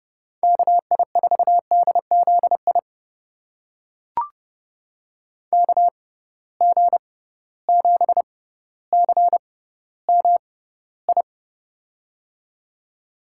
Morse code, heard as a keyed tone, about 32 wpm, spaced as for 16 wpm, 700 Hz. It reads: KI4DZI KG7CMI
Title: USA Call Signs - 6 Characters - 6x Character Spacing - 30wpm